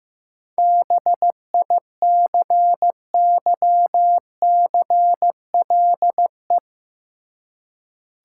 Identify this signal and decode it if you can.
Morse code, heard as a keyed tone, about 15 wpm, 700 Hz. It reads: BICYCLE
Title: Seven Letter Words - 15wpm